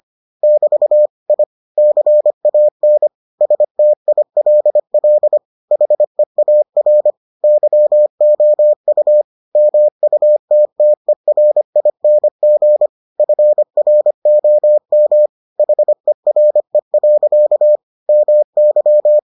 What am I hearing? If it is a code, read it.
Morse code, heard as a keyed tone, about 25 wpm, 600 Hz. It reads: <BT> I CAN STILL HEAR YOU MUTTERING FROM HERE. MY